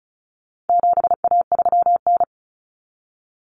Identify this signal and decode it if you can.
Morse code, heard as a keyed tone, about 34 wpm, 700 Hz. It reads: 7A3N
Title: Common Cut Numbers - 4-Digit 35wpm